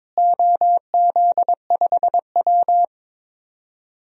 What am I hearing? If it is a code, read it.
Morse code, heard as a keyed tone, about 22 wpm, 700 Hz. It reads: OZ5W
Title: DXCC-100 Call Signs - 4 Characters - Repeated 3x - 22wpm